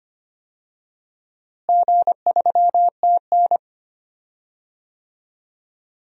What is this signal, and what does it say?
Morse code, heard as a keyed tone, about 25 wpm, 700 Hz. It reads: G3TN
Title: DXCC-100 Call Signs - 4 Characters - Repeated 3x - 25wpm